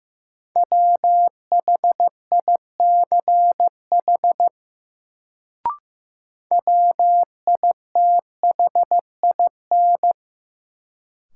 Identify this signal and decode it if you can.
Morse code, heard as a keyed tone, about 15 wpm, 700 Hz. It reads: WHICH WITHIN
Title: Sets of 2 Words - New Word Drill - 15wpm